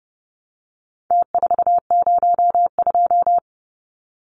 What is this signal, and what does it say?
Morse code, heard as a keyed tone, about 30 wpm, 700 Hz. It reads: T402
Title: Numbers - Contest Style - 4-Digit 30wpm